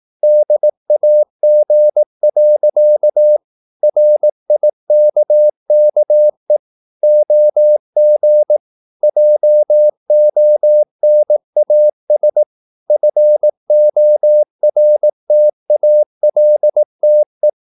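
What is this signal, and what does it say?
Morse code, heard as a keyed tone, about 18 wpm, 600 Hz. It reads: DAG. RIKKE OG JONAS FORTALTE
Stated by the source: Slow (QRS) Morse practice at 18 wpm, CW med Gnister.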